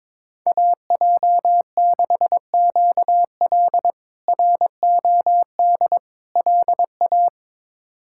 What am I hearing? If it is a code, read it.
Morse code, heard as a keyed tone, about 22 wpm, 700 Hz. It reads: AJ6QL ROD LA
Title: K1USN SST QSO Practice - 22wpm